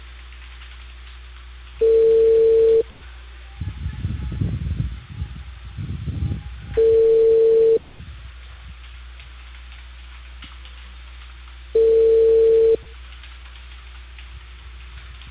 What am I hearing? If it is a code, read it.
Ringback tone (China, 450 Hz).